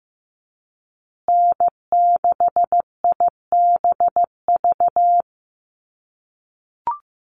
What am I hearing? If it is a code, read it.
Morse code, heard as a keyed tone, about 15 wpm, 700 Hz. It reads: N6IBV
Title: USA Call Signs Repeated 2x 15wpm